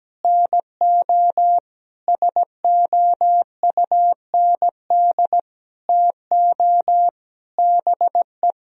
Morse code, heard as a keyed tone, about 17 wpm, 700 Hz. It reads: NO SOUND TO BE
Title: Sentences from Top 1000 Words - Review Part 1 - 17wpm